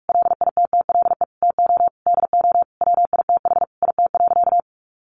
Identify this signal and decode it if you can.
Morse code, heard as a keyed tone, about 44 wpm, 700 Hz. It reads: LITTLE TO DO WITH IT.
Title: Sentences from Top 1500 Words 45wpm